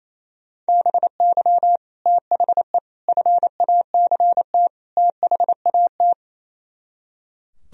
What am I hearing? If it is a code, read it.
Morse code, heard as a keyed tone, about 34 wpm, 700 Hz. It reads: BY THE FACT THAT
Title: Speed-Racing - Sets of 4 Words 28wpm